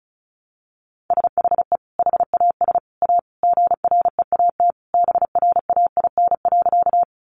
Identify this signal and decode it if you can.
Morse code, heard as a keyed tone, about 34 wpm, 700 Hz. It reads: SHE HAS A GREAT BRAIN.
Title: Sentences from Top 1100 Words 35wpm